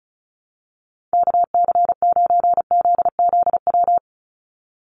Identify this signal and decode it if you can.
Morse code, heard as a keyed tone, about 34 wpm, 700 Hz. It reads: KC9ZZW
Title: USA Call Signs 35wpm